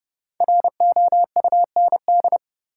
Morse code, heard as a keyed tone, about 30 wpm, 700 Hz. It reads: ROUND
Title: Sets of 5 Words Encore - New Word Drill - 30wpm